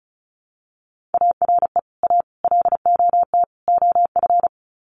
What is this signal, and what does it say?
Morse code, heard as a keyed tone, about 34 wpm, 700 Hz. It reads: ARE A LOT OF
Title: Sets of 4 Words 35wpm